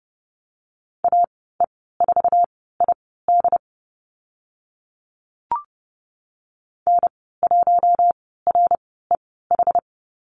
Morse code, heard as a keyed tone, about 29 wpm, spaced as for 17 wpm, 700 Hz. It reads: AE4ID N1REH
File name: Lesson 14g - Number 2 - Call Signs 30wpm